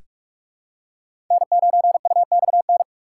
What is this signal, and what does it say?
Morse code, heard as a keyed tone, about 44 wpm, 700 Hz. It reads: N9UXN
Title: USA Call Signs - 5 Characters - 6x Character Spacing - 45wpm